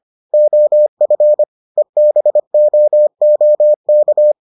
Morse code, heard as a keyed tone, about 25 wpm, 600 Hz. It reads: OF EBOOK